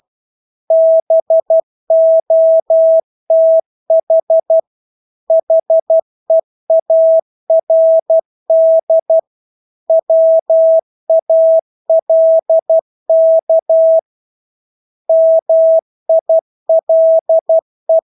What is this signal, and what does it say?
Morse code, heard as a keyed tone, about 12 wpm, 650 Hz. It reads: BOTH HEARD WALK MILE
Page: Engelske ord 12 wpm